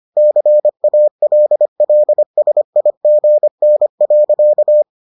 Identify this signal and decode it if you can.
Morse code, heard as a keyed tone, about 25 wpm, 600 Hz. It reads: CALLSIGN.